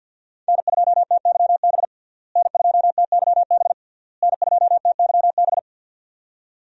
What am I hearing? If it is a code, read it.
Morse code, heard as a keyed tone, about 52 wpm, 700 Hz. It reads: N2TYB N2TYB N2TYB
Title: USA Call Signs - 5 Characters - Repeated 3x - 50wpm